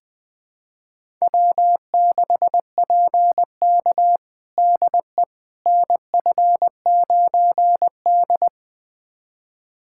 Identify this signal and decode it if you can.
Morse code, heard as a keyed tone, about 20 wpm, 700 Hz. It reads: W6PK DE NF9D
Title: USA Callsign DE Callsign - 4 Characters - 20wpm